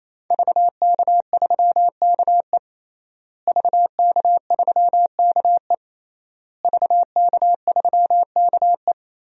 Morse code, heard as a keyed tone, about 27 wpm, 700 Hz. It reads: VK3KE VK3KE VK3KE
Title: DXCC-100 Call Signs - 5 Characters - Repeated 3x - 28wpm